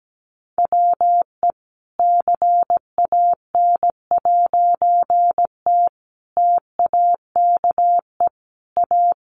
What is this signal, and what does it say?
Morse code, heard as a keyed tone, about 17 wpm, 700 Hz. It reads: WE CAN'T TAKE A
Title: A Sound Of Thunder 17wpm